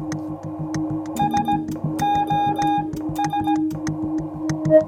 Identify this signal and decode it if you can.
Morse code, heard as a keyed tone, about 17 wpm, 820 Hz. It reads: S O S